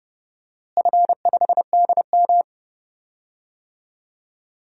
Morse code, heard as a keyed tone, about 30 wpm, 700 Hz. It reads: F5DM
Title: DXCC-100 Call Signs - 4 Characters - Repeated 3x - 30wpm